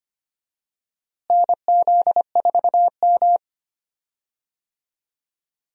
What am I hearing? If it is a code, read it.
Morse code, heard as a keyed tone, about 25 wpm, 700 Hz. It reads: NZ4M
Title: USA Call Signs - 4 Characters - Repeated 2x - 25wpm